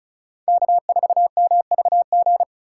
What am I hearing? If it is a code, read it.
Morse code, heard as a keyed tone, about 44 wpm, 700 Hz. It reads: K4MVG